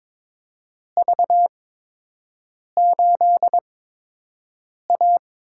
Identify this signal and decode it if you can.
Morse code, heard as a keyed tone, about 22 wpm, spaced as for 6 wpm, 700 Hz. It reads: V8A
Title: USA Call Signs - 4 Characters - 8x Character Spacing - 22wpm